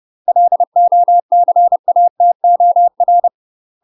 Morse code, heard as a keyed tone, about 30 wpm, 700 Hz. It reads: LOCATOR
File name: UK MASTER QSOs TRAINING / 30WPM / UK MASTER QSOs 0008